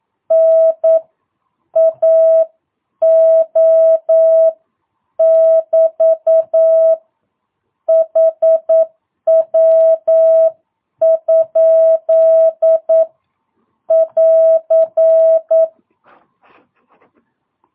Morse code, heard as a keyed tone, about 9 wpm, 650 Hz. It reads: N AO <BT> HW? <AR>
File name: jh8cbh / 2022cw